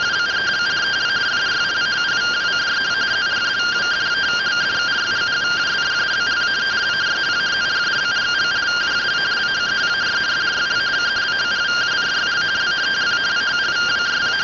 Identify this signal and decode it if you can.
Two-tone teleprinter signal, about 170 Hz shift.